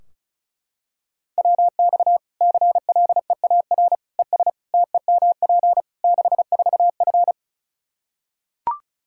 Morse code, heard as a keyed tone, about 34 wpm, 700 Hz. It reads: WX CLEAR ES TEMP 64F
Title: K1USN SST QSO Practice 35wpm